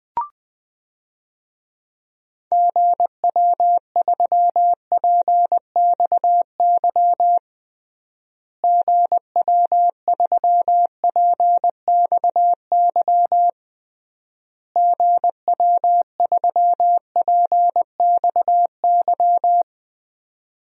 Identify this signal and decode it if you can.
Morse code, heard as a keyed tone, about 20 wpm, 700 Hz. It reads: GW3PXY GW3PXY GW3PXY
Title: DXCC-100 Call Signs - 6 Characters - Repeated 3x - 20wpm